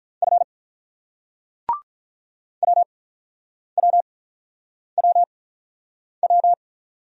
Morse code, heard as a keyed tone, about 44 wpm, spaced as for 9 wpm, 700 Hz. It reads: F WWWW
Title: Speed-Racing Single Letters 35wpm